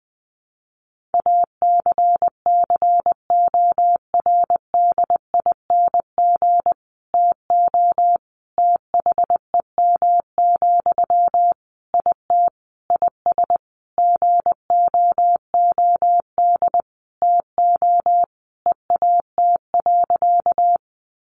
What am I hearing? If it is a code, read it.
Morse code, heard as a keyed tone, about 20 wpm, 700 Hz. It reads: ACCORDING TO THEM, IT IS GOOD TO EAT.
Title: Sentences from Top 800 Words 20wpm